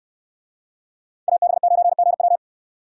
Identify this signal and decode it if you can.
Morse code, heard as a keyed tone, about 66 wpm, 700 Hz. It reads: AB9GW